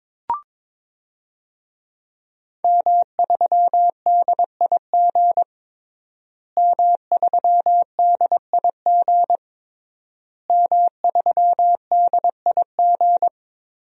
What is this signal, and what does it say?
Morse code, heard as a keyed tone, about 22 wpm, 700 Hz. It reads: M3DIG M3DIG M3DIG
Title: DXCC-100 Call Signs - 5 Characters - Repeated 3x - 22wpm